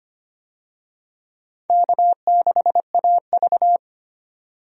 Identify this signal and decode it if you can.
Morse code, heard as a keyed tone, about 25 wpm, 700 Hz. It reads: K6AV
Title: USA Call Signs - 4 Characters - 25wpm